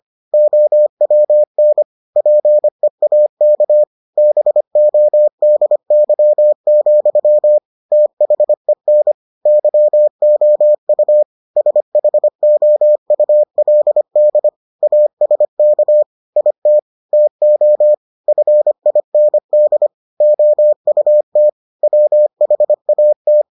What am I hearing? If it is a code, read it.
Morse code, heard as a keyed tone, about 25 wpm, 600 Hz. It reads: OWN PEAK BODY, THEN YOU SHOULD ASK IT TO FIND OUT WHAT